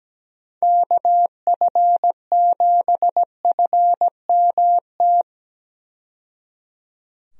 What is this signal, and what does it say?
Morse code, heard as a keyed tone, about 17 wpm, 700 Hz. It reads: KF7FMT
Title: USA Call Signs - 6 Characters - 17wpm